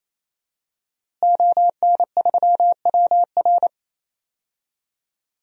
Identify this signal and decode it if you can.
Morse code, heard as a keyed tone, about 27 wpm, 700 Hz. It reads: ON3WR
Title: DXCC-100 Call Signs - 5 Characters - Repeated 3x - 28wpm